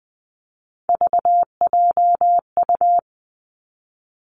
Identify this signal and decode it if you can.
Morse code, heard as a keyed tone, about 20 wpm, 700 Hz. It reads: VJU
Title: Three Challenging Letter-Number Combinations 20wpm